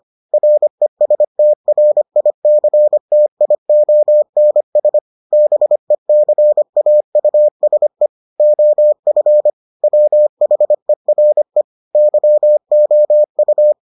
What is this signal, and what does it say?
Morse code, heard as a keyed tone, about 25 wpm, 600 Hz. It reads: RESTRICTIONS BECAUSE OF WHERE YOU